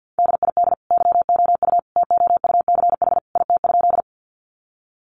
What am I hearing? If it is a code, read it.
Morse code, heard as a keyed tone, about 50 wpm, 700 Hz. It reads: DID YOU TOUCH IT?
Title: Sentences from Top 1200 Words 50wpm